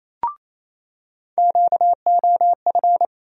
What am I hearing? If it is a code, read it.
Morse code, heard as a keyed tone, about 28 wpm, 700 Hz. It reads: QOF
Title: USA Call Sign Postfixes - 3 Letters 28wpm